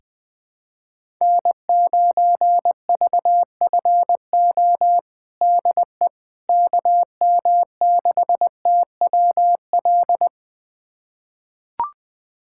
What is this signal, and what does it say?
Morse code, heard as a keyed tone, about 21 wpm, 700 Hz. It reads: N9VFO DE KM6TWL
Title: USA Callsign DE Callsign 20wpm